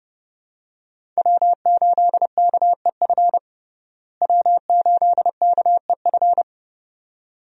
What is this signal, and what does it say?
Morse code, heard as a keyed tone, about 31 wpm, 700 Hz. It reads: W8KEF W8KEF
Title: USA Call Signs - 5 Characters - Repeated 2x - 30wpm